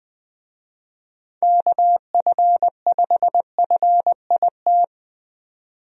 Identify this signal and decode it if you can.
Morse code, heard as a keyed tone, about 20 wpm, 700 Hz. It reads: KF5FIT
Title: USA Call Signs - 6 Characters - 20wpm